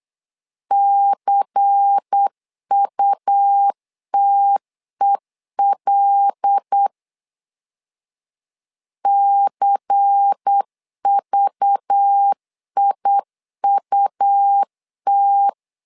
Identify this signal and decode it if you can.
Morse code, heard as a keyed tone, about 8 wpm, 800 Hz. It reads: CUTEL CVIUT
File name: LEZIONI / 5 LBVC / 5 LBVC 8wpm 2935